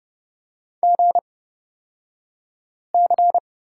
Morse code, heard as a keyed tone, about 31 wpm, spaced as for 5 wpm, 700 Hz. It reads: GC
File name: Lesson 36d - BT - Two Characters 30wpm